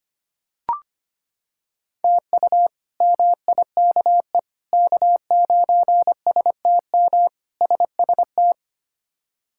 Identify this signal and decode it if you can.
Morse code, heard as a keyed tone, about 25 wpm, 700 Hz. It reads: TU MIKE K9STM SST
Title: K1USN SST QSO Practice - 25wpm